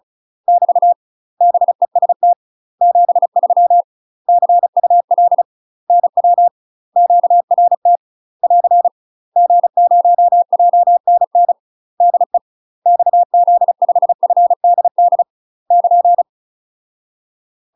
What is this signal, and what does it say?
Morse code, heard as a keyed tone, about 36 wpm, 700 Hz. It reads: <BT> BEST 73 CUL NW QRT <AR> G0JNN DE XZ5FDD <KN>
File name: UK MASTER QSOs TRAINING / 35WPM / UK MASTER QSOs 0044